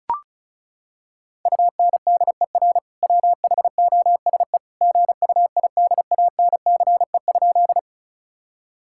Morse code, heard as a keyed tone, about 36 wpm, 700 Hz. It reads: UNDER WHOSE GUIDANCE?
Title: Sentences from Top 1600 Words 35wpm